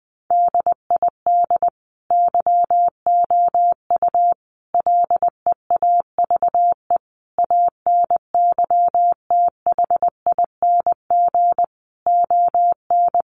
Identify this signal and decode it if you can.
Morse code, heard as a keyed tone, about 20 wpm, 700 Hz. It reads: DID YOU LEAVE ANYTHING ON
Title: Sentences from Top 1000 Words 20wpm